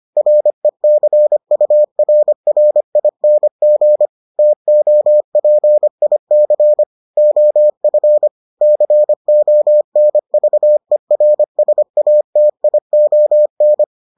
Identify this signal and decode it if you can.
Morse code, heard as a keyed tone, about 25 wpm, 600 Hz. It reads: RECURRING TOPIC OF CONVERSATION